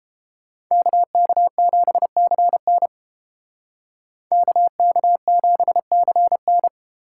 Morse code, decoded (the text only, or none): KK7CN KK7CN